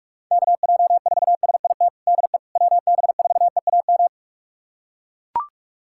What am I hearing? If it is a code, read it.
Morse code, heard as a keyed tone, about 41 wpm, 700 Hz. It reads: KJ4SIT DE WB4EAM
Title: USA Callsign DE Callsign - 6 Characters - 45wpm